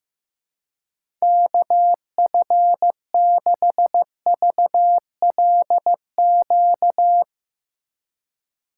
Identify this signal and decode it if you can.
Morse code, heard as a keyed tone, about 15 wpm, 700 Hz. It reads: KF6VLQ
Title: USA Call Signs - 6 Characters - 3x Character Spacing - 15wpm